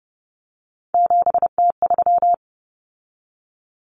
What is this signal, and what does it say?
Morse code, heard as a keyed tone, about 30 wpm, 700 Hz. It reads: 7T3
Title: Common Cut Numbers - 3-Digit 30wpm